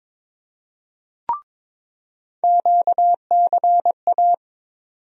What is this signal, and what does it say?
Morse code, heard as a keyed tone, about 21 wpm, 700 Hz. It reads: QCA
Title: USA Call Sign Postfixes - 3 Letters 22wpm